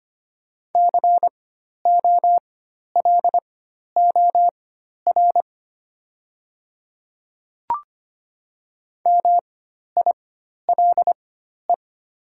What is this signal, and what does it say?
Morse code, heard as a keyed tone, about 25 wpm, spaced as for 12 wpm, 700 Hz. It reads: COLOR MILE
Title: 4x Character Spacing - Top 300 Words - 25wpm